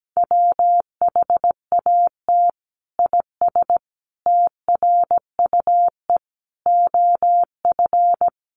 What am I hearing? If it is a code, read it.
Morse code, heard as a keyed tone, about 17 wpm, 700 Hz. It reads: WHAT IS TRUE OF
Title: Sentences from Top 1400 Words 17wpm